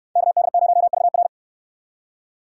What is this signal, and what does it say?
Morse code, heard as a keyed tone, about 64 wpm, 700 Hz. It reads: KD0VR